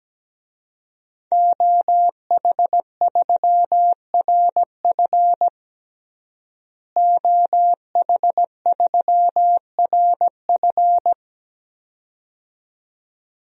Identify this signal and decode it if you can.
Morse code, heard as a keyed tone, about 17 wpm, 700 Hz. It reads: OH3RF OH3RF
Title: DXCC-100 Call Signs - 5 Characters - Repeated 2x - 17wpm